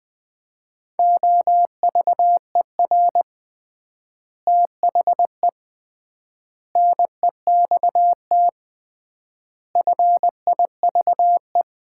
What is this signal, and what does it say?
Morse code, heard as a keyed tone, about 20 wpm, 700 Hz. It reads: OVER THE NEXT FIVE
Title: 3x Word-Spacing - Sets of 5 Words - 20wpm